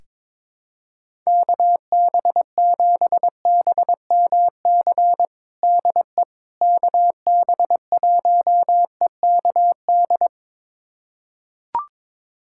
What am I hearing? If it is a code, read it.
Morse code, heard as a keyed tone, about 22 wpm, 700 Hz. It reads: KB7BMC DE KB1EKD
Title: USA Callsign DE Callsign - 6 Characters - 22wpm